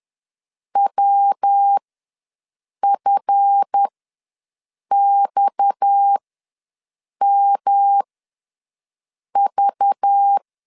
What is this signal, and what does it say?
Morse code, heard as a keyed tone, about 10 wpm, spaced as for 6 wpm, 800 Hz. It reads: WFXMV